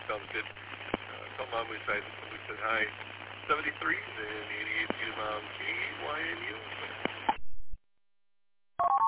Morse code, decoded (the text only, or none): none